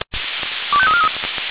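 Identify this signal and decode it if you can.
Five-tone selective call (ZVEI), heard as quick runs of five tones, tones of about 70 ms. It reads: 25343
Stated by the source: ZVEI five-tone selective call received on an SDR.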